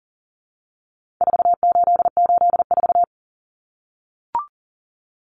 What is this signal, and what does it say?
Morse code, heard as a keyed tone, about 37 wpm, 700 Hz. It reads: 4884
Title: Common Cut Numbers - 4-Digit 40wpm